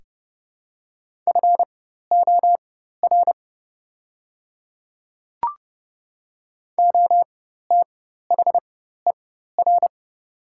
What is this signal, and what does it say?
Morse code, heard as a keyed tone, about 32 wpm, spaced as for 14 wpm, 700 Hz. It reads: FOR OTHER
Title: 4x Character Spacing - Top 100 Words - 30wpm